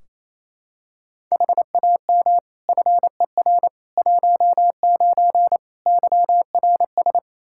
Morse code, heard as a keyed tone, about 28 wpm, 700 Hz. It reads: HAM FER 19 YRS
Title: Practice QSO - Exchange 3 - More Info - 28wpm